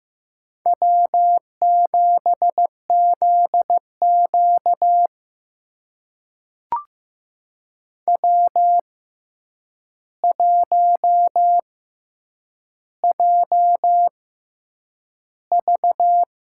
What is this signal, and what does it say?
Morse code, heard as a keyed tone, about 15 wpm, spaced as for 8 wpm, 700 Hz. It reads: W7ZQ W1JV
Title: USA Call Signs - 4 Characters - 6x Character Spacing - 15wpm